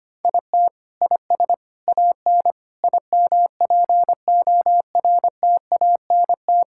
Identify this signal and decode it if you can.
Morse code, heard as a keyed tone, about 25 wpm, 700 Hz. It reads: IT IS AN IMPORTANT
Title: Sentences from Top 1000 Words - Review Part 5 - 25wpm